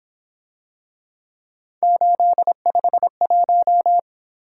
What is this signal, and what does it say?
Morse code, heard as a keyed tone, about 26 wpm, 700 Hz. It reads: 851
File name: Speed-Racing - Three Letter-Number Combinations - 22wpm